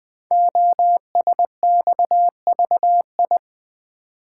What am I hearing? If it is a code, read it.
Morse code, heard as a keyed tone, about 20 wpm, 700 Hz. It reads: OSXVI
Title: Code Groups - 5 Letters - 20wpm